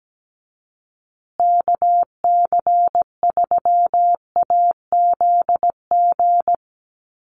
Morse code, heard as a keyed tone, about 17 wpm, 700 Hz. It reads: KC3AZG
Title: USA Call Signs Repeated 2x 17wpm